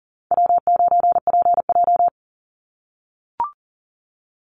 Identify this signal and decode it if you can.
Morse code, heard as a keyed tone, about 40 wpm, 700 Hz. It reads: W9PJ
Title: Easy Call Signs - 40wpm